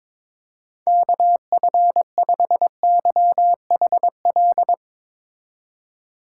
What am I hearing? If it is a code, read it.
Morse code, heard as a keyed tone, about 22 wpm, 700 Hz. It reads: KF5YHL
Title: USA Call Signs - 6 Characters - 2x Character Spacing - 22wpm